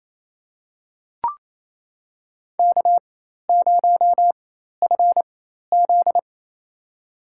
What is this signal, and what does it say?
Morse code, heard as a keyed tone, about 28 wpm, spaced as for 13 wpm, 700 Hz. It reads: K0FZ
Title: USA Call Signs - 4x Character Spacing - 28wpm